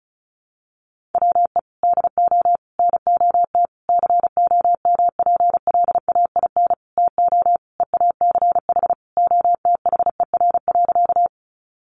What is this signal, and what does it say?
Morse code, heard as a keyed tone, about 34 wpm, 700 Hz. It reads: WE DO NOT COMPLAIN TO EACH OTHER.
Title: Sentences from Top 1300 Words 35wpm